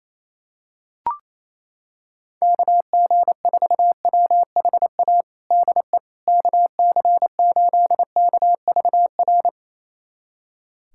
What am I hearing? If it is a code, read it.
Morse code, heard as a keyed tone, about 27 wpm, 700 Hz. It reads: KG4WHA DE KC8KVR
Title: USA Callsign DE Callsign - 6 Characters - 28wpm